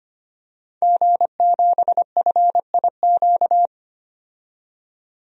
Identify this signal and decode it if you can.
Morse code, heard as a keyed tone, about 25 wpm, 700 Hz. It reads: G7FIQ
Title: DXCC-100 Call Signs - 5 Characters - 25wpm